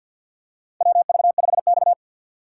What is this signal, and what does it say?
Morse code, heard as a keyed tone, about 47 wpm, 700 Hz. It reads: WV5X
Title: USA Call Signs - 4x Character Spacing - 50wpm